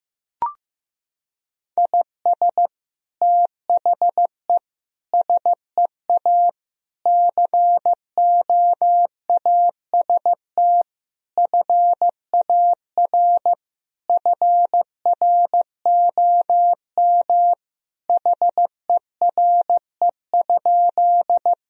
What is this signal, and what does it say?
Morse code, heard as a keyed tone, about 16 wpm, 700 Hz. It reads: IS THE SEA COAST FAR FROM HERE?
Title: Sentences from Top 1000 Words - Review Part 4 - 15wpm